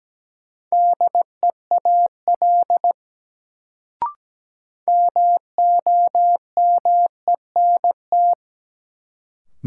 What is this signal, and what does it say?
Morse code, heard as a keyed tone, about 17 wpm, 700 Hz. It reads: DEAL MOMENT
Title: Sets of 3 Words - New Word Drill - 17wpm